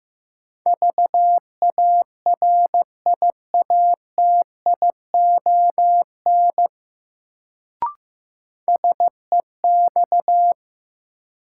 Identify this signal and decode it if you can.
Morse code, heard as a keyed tone, about 15 wpm, 700 Hz. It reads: VARIATION SEX
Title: Sentences from Top 1600 Words - New Word Drill - 15wpm